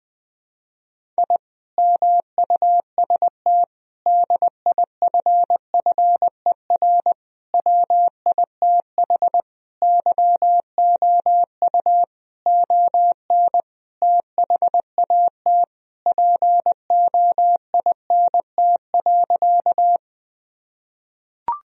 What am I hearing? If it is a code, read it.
Morse code, heard as a keyed tone, about 20 wpm, 700 Hz. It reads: I MUST DIFFER WITH YOU ON THAT POINT.
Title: Sentences from Top 1000 Words - Review Part 1 - 20wpm